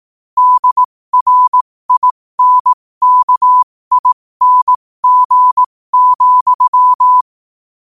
Morse code, heard as a keyed tone, about 18 wpm, spaced as for 15 wpm, 1000 Hz. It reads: DRINKING,
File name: qotdfiles / 15wpm